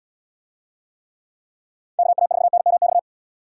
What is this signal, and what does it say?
Morse code, heard as a keyed tone, about 69 wpm, 700 Hz. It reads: KA4NMX